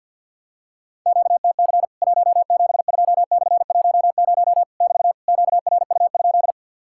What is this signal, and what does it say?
Morse code, heard as a keyed tone, about 54 wpm, 700 Hz. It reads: QTC 172/10 <BT> QRU?
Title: Worked All Europe DX Contest - QTC Exchange with Repeats - 50wpm